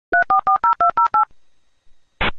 DTMF, heard as telephone touch tones, tones of about 110 ms. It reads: 3440208